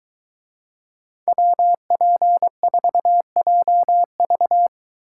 Morse code, heard as a keyed tone, about 22 wpm, 700 Hz. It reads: WP4JV